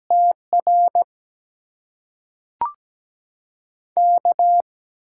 Morse code, heard as a keyed tone, about 17 wpm, spaced as for 5 wpm, 700 Hz. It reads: TRK
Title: USA Call Signs - 6x Character Spacing - 17wpm